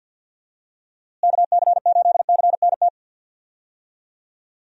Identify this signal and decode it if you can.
Morse code, heard as a keyed tone, about 48 wpm, 700 Hz. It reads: KC8CNT
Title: USA Call Signs - 6 Characters - Repeated 2x - 50wpm